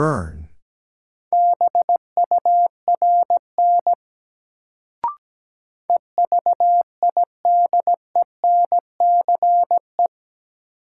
Morse code, heard as a keyed tone, about 17 wpm, 700 Hz. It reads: BURN EVIDENCE
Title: Sentences from Top 1300 Words - New Word Drill Alt - 17wpm